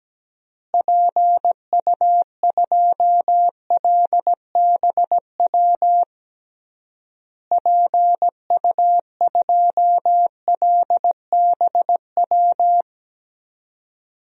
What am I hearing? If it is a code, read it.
Morse code, heard as a keyed tone, about 17 wpm, 700 Hz. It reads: PU2LBW PU2LBW
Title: DXCC-100 Call Signs - 6 Characters - Repeated 3x - 17wpm